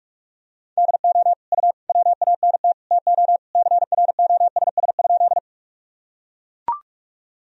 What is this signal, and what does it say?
Morse code, heard as a keyed tone, about 41 wpm, 700 Hz. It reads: DO U WANT TO CROSS?
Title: Sentences from Top 300 Words and Abbreviations 45wpm